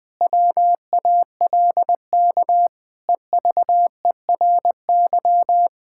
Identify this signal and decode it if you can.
Morse code, heard as a keyed tone, about 20 wpm, 700 Hz. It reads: WALK EVERY
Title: Difficult Sentences with Hints - 20wpm